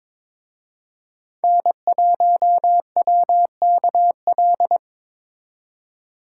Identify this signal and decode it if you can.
Morse code, heard as a keyed tone, about 26 wpm, 700 Hz. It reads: N1WKL
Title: Speed Racing - USA 1x3 Call Signs 22wpm